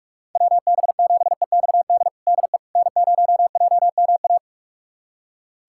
Morse code, heard as a keyed tone, about 46 wpm, 690 Hz. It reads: WB7EXD DE N0JMA
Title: USA Callsign DE Callsign 45wpm